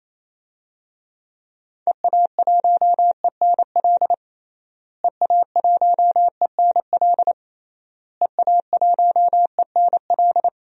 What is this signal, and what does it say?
Morse code, heard as a keyed tone, about 28 wpm, 700 Hz. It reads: EA1ENL EA1ENL EA1ENL
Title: DXCC-100 Call Signs - 6 Characters - Repeated 3x - 28wpm